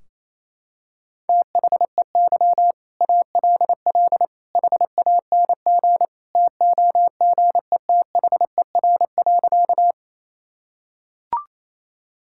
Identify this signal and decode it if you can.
Morse code, heard as a keyed tone, about 28 wpm, 700 Hz. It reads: THEY ALL HANG TOGETHER.
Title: Sentences from Top 1600 Words 28wpm